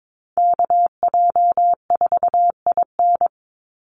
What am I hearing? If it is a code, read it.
Morse code, heard as a keyed tone, about 22 wpm, 700 Hz. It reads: KJ4IN
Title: USA Call Signs 22wpm